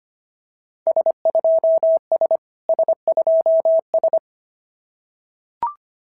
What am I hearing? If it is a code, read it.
Morse code, heard as a keyed tone, about 25 wpm, 690 Hz. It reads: S2S S2S
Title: SOTA QSO Practice 25wpm